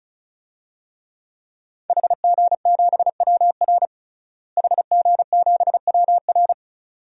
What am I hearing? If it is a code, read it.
Morse code, heard as a keyed tone, about 36 wpm, 700 Hz. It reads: HG7WR HG7WR
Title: DXCC-100 Call Signs - 5 Characters - Repeated 2x - 35wpm